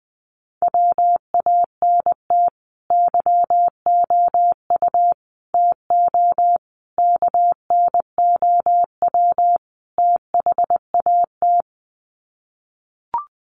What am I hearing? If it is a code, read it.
Morse code, heard as a keyed tone, about 20 wpm, 700 Hz. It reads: WANT YOU TO KNOW THAT
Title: Sets of 5 Words Encore 20wpm